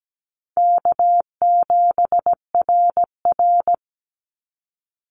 Morse code, heard as a keyed tone, about 17 wpm, 700 Hz. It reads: K7RR
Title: Easy Call Signs Repeated 2x - 17wpm